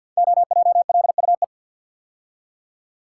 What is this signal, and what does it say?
Morse code, heard as a keyed tone, about 46 wpm, 700 Hz. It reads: OJLUE
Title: Code Groups - 5 Letter:Numbers - 50wpm